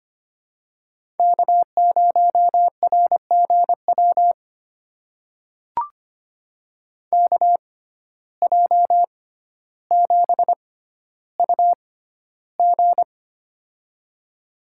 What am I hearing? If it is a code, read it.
Morse code, heard as a keyed tone, about 25 wpm, spaced as for 13 wpm, 700 Hz. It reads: K0RGW KJ7UG
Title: USA Call Signs - 5 Characters - 6x Character Spacing - 25wpm